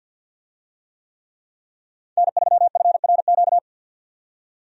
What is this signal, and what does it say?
Morse code, heard as a keyed tone, about 50 wpm, 700 Hz. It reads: N3FRQ
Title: USA Call Signs - 5 Characters - 50wpm